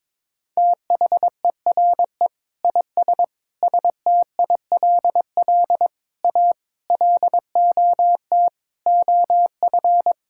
Morse code, heard as a keyed tone, about 22 wpm, 700 Hz. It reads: THERE IS STILL A LOT OF
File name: Sentences with 9-Word Phrases - 22wpm